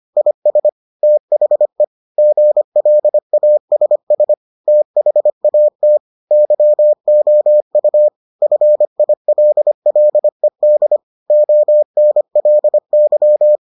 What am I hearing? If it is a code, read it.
Morse code, heard as a keyed tone, about 25 wpm, 600 Hz. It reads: IS THE GLASS THAT YOU FILLED ONLY